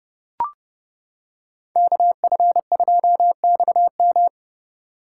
Morse code, heard as a keyed tone, about 30 wpm, 700 Hz. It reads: KF2XM